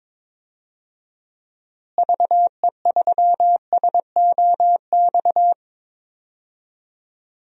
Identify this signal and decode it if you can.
Morse code, heard as a keyed tone, about 22 wpm, 700 Hz. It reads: VE3SOX
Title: DXCC-100 Call Signs - 6 Characters - Repeated 3x - 22wpm